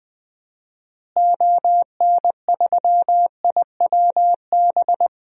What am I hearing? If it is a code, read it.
Morse code, heard as a keyed tone, about 20 wpm, 700 Hz. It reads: ON3IWB
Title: DXCC-100 Call Signs - 6 Characters - Repeated 3x - 20wpm